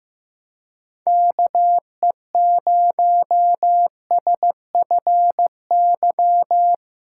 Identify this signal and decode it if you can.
Morse code, heard as a keyed tone, about 15 wpm, 700 Hz. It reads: KE0SFY
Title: USA Call Signs - 6 Characters - 3x Character Spacing - 15wpm